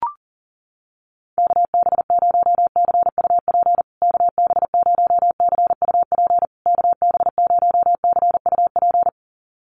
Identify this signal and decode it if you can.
Morse code, heard as a keyed tone, about 40 wpm, 700 Hz. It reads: KB0CUP KB0CUP KB0CUP
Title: USA Call Signs Repeated 3x 40wpm